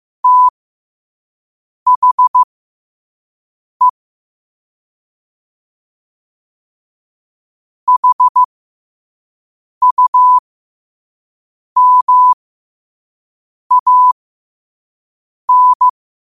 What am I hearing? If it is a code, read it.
Morse code, heard as a keyed tone, about 15 wpm, spaced as for 5 wpm, 1000 Hz. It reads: THE HUMAN
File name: qotdfiles / 5wpm